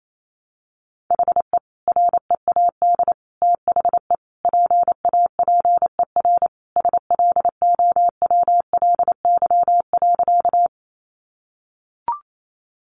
Morse code, heard as a keyed tone, about 28 wpm, 700 Hz. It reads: HE READ THE PAPER SLOWLY.
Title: Sentences from Top 1100 Words 28wpm